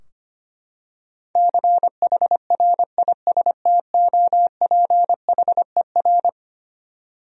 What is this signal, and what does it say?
Morse code, heard as a keyed tone, about 25 wpm, 700 Hz. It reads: CHRISTOPHER
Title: USA Common Names - 25wpm